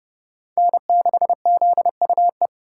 Morse code, heard as a keyed tone, about 38 wpm, 700 Hz. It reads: N6ZUE
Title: Speed Racing - USA Call Signs 30wpm